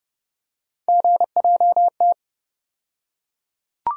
Morse code, heard as a keyed tone, about 30 wpm, 700 Hz. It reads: GJT
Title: USA Call Sign Postfixes - 3 Letters 30wpm